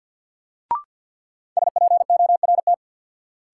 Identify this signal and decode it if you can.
Morse code, heard as a keyed tone, about 51 wpm, 700 Hz. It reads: SPORT
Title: Sentences from Top 1000 Words - Review Part 4 - New Word Drill - 50wpm